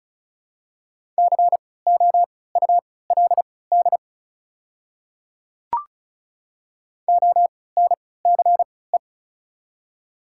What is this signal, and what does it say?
Morse code, heard as a keyed tone, about 36 wpm, spaced as for 20 wpm, 700 Hz. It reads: COULD ONCE
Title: 3x Character Spacing - Top 300 Words - 35wpm